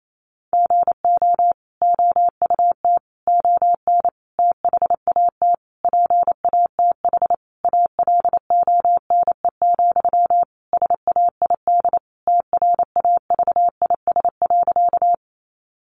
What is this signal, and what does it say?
Morse code, heard as a keyed tone, about 28 wpm, 700 Hz. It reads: GO OUT ON THAT PATH ALONE, SAID TRAVIS.
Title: A Sound Of Thunder 28wpm